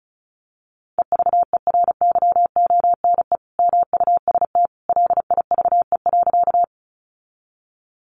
Morse code, heard as a keyed tone, about 34 wpm, 700 Hz. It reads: EVERYONE MUST LIVE.
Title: Sentences from Top 800 Words 35wpm